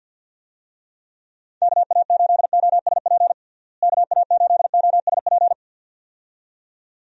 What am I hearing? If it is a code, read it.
Morse code, heard as a keyed tone, about 51 wpm, 700 Hz. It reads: KA8OSP KA8OSP
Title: USA Call Signs - 6 Characters - Repeated 2x - 50wpm